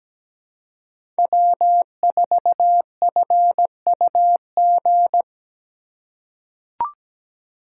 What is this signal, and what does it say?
Morse code, heard as a keyed tone, about 17 wpm, 700 Hz. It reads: W4FUG
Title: USA Call Signs - 2x Character Spacing - 17wpm